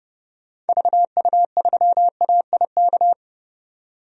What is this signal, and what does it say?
Morse code, heard as a keyed tone, about 31 wpm, 700 Hz. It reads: VU3AIK
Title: DXCC-100 Call Signs - 6 Characters - 30wpm